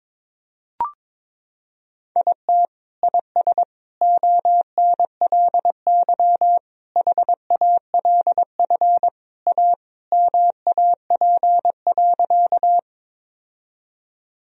Morse code, heard as a keyed tone, about 22 wpm, 700 Hz. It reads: IT IS ONLY HALF A MAP.
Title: Sentences from Top 1000 Words - Review Part 2 - 22wpm